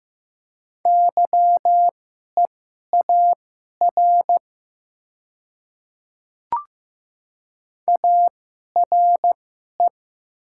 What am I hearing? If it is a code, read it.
Morse code, heard as a keyed tone, about 15 wpm, spaced as for 11 wpm, 700 Hz. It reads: YEAR ARE
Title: 2x Character Spacing - Top 300 Words - 15wpm